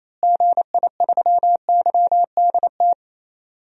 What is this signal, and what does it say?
Morse code, heard as a keyed tone, about 28 wpm, 700 Hz. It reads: GI3YDT
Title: DXCC-100 Call Signs - 6 Characters - Repeated 2x - 28wpm